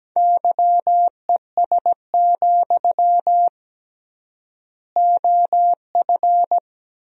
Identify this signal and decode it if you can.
Morse code, heard as a keyed tone, about 17 wpm, 700 Hz. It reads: YES, OF
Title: 3x Word-Spacing - Sentences from Top 500 Words - 17wpm